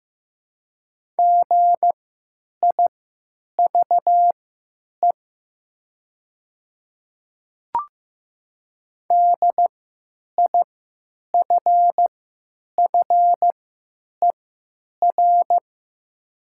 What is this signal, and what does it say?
Morse code, heard as a keyed tone, about 15 wpm, spaced as for 9 wpm, 700 Hz. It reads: GIVE DIFFER
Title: 3x Character Spacing - Top 500 Words - 15wpm